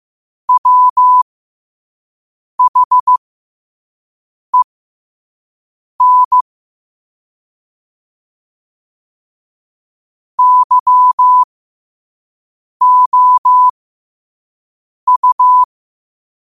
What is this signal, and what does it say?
Morse code, heard as a keyed tone, about 15 wpm, spaced as for 5 wpm, 1000 Hz. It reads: WHEN YOU